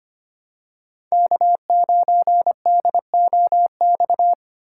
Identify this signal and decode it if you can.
Morse code, heard as a keyed tone, about 25 wpm, 700 Hz. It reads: K9DOX
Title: USA Call Signs - 5 Characters - 6x Character Spacing - 25wpm